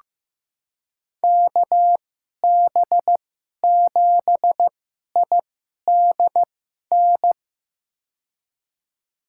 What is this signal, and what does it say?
Morse code, heard as a keyed tone, about 15 wpm, spaced as for 11 wpm, 700 Hz. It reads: KB7IDN
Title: USA Call Signs - 6 Characters - 2x Character Spacing - 15wpm